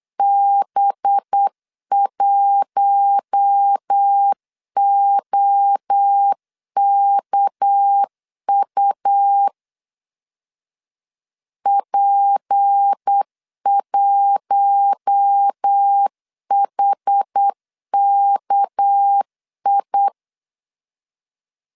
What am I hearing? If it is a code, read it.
Morse code, heard as a keyed tone, about 8 wpm, 800 Hz. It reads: B1OKU P1HKI
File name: LEZIONI / 6 PKX0 / 6 PK 8wpm 3003